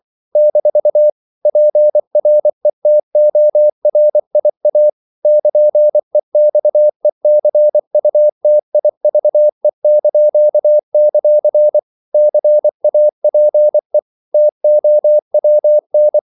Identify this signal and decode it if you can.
Morse code, heard as a keyed tone, about 24 wpm, 600 Hz. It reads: - PRETORIA <KN>EXECUTIVE); CAPE TOWN
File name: CW / Practice / NationalCapitals24